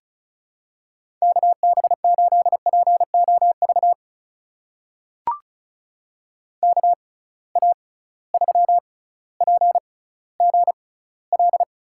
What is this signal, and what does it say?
Morse code, heard as a keyed tone, about 36 wpm, spaced as for 18 wpm, 700 Hz. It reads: KB8POV KA3PGL
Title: USA Call Signs - 6 Characters - 6x Character Spacing - 35wpm